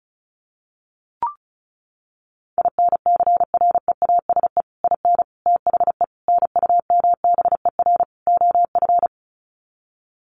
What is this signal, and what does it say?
Morse code, heard as a keyed tone, about 34 wpm, 700 Hz. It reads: INCREASE IN THE NUMBER OF
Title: Sets of 5 Words 35wpm